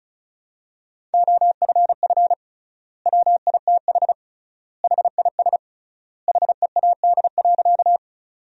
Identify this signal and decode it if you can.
Morse code, heard as a keyed tone, about 36 wpm, 700 Hz. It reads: OFF WITH HIS HEAD.
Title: 3x Word-Spacing - Sentences from Top 300 Words - 35wpm